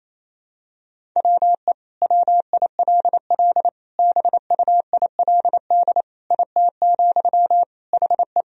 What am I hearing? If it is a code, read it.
Morse code, heard as a keyed tone, about 28 wpm, 700 Hz. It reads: WE WILL BUILD IT, HE
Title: Sentences from Top 1000 Words - Review Part 1 - 28wpm